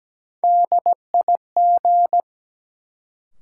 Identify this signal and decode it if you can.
Morse code, heard as a keyed tone, about 17 wpm, 700 Hz. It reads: DIG
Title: Sentences from Top 1600 Words - New Word Drill - 17wpm